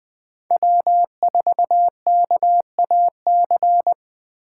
Morse code, heard as a keyed tone, about 20 wpm, 700 Hz. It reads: W4KAC
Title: 2023 SOTA Chaser Call Signs - 20wpm